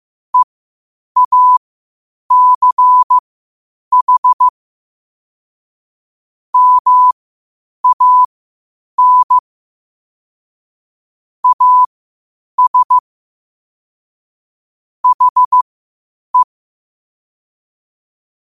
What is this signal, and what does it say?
Morse code, heard as a keyed tone, about 15 wpm, spaced as for 8 wpm, 1000 Hz. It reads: EACH MAN AS HE
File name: qotdfiles / 8wpm